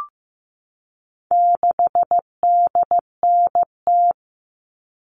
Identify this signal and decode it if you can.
Morse code, heard as a keyed tone, about 15 wpm, 700 Hz. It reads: 6DNT
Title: All Cut Numbers - 4-Digit 15wpm